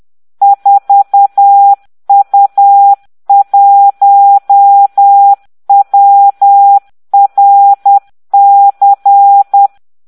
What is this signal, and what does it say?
Morse code, heard as a keyed tone, about 10 wpm, 800 Hz. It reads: 4U1WRC